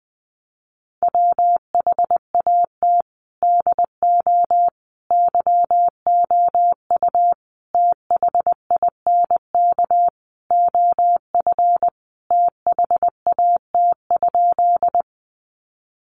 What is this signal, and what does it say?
Morse code, heard as a keyed tone, about 20 wpm, 700 Hz. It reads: WHAT DO YOU THINK OF THAT?
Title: Sentences from Top 200 Words 20wpm